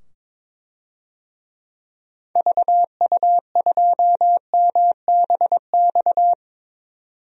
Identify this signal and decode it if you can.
Morse code, heard as a keyed tone, about 21 wpm, 700 Hz. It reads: VU2MBX
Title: DXCC-100 Call Signs - 6 Characters - 22wpm